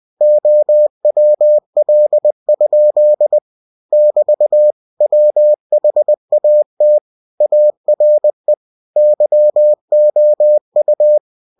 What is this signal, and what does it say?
Morse code, heard as a keyed tone, about 20 wpm, 600 Hz. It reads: OWL? <BT> WHAT ARE YOU
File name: cw-books / winnie-the-pooh / winnie-20wpm-page-0051